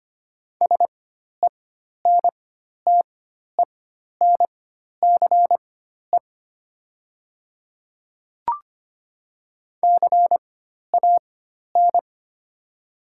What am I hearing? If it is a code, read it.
Morse code, heard as a keyed tone, about 25 wpm, spaced as for 12 wpm, 700 Hz. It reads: SENTENCE CAN
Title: 4x Character Spacing - Top 200 Words - 25wpm